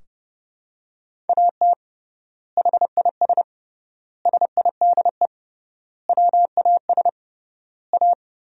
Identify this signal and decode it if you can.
Morse code, heard as a keyed tone, about 28 wpm, 700 Hz. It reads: AT HIS SIDE WAS A
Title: 3x Word-Spacing - Sentences from Top 500 Words - 30wpm